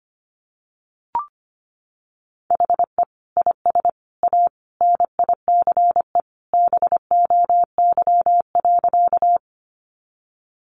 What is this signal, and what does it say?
Morse code, heard as a keyed tone, about 25 wpm, 700 Hz. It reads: HE IS A NICE BOY.
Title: Sentences from Top 1100 Words 25wpm